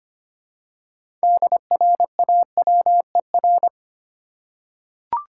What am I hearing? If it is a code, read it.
Morse code, heard as a keyed tone, about 25 wpm, 700 Hz. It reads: DRAWER
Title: Six Letter Words - 25wpm